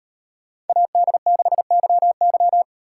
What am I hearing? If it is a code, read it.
Morse code, heard as a keyed tone, about 36 wpm, 700 Hz. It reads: AD6YY